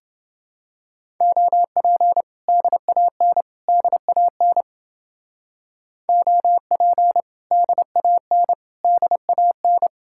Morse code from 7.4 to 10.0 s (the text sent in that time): DAN DAN